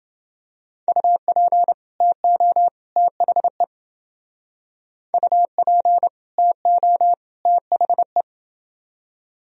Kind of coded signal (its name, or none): Morse code